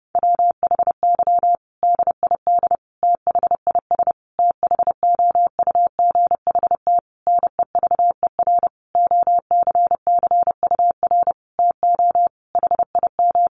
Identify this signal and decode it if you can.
Morse code, heard as a keyed tone, about 30 wpm, 700 Hz. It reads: WHY DID THIS THOUGHT NEVER OCCUR TO HIM
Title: Sentences from Top 900 Words 30wpm